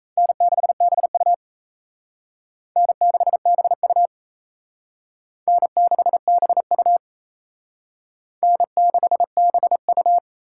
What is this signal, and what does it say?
Morse code, heard as a keyed tone, about 34 wpm, 700 Hz. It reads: N6BU N6BU N6BU N6BU
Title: Speed Racing - USA 1x2 Call Signs 28wpm